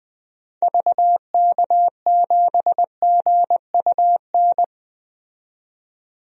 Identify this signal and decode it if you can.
Morse code, heard as a keyed tone, about 20 wpm, 700 Hz. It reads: VK7GUN
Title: DXCC-100 Call Signs - 6 Characters - 20wpm